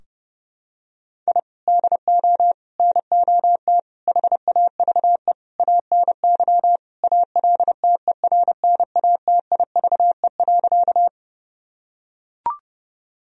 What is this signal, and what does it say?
Morse code, heard as a keyed tone, about 30 wpm, 700 Hz. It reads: I DO NOT HAVE ANY ALTERNATIVE.
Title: Sentences from Top 1000 Words - Review Part 4 - 30wpm